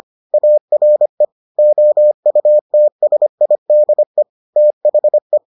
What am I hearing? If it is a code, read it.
Morse code, heard as a keyed tone, about 25 wpm, 600 Hz. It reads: ARE OUTSIDE THE